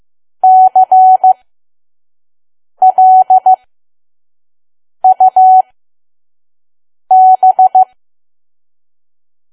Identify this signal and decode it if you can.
Morse code, heard as a keyed tone, about 15 wpm, spaced as for 5 wpm, 750 Hz. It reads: CLUB
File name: Archive / 5 WPM files / PM